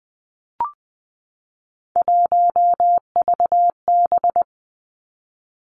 Morse code, heard as a keyed tone, about 20 wpm, 700 Hz. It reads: 1VB
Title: Three Challenging Letter-Number Combinations 20wpm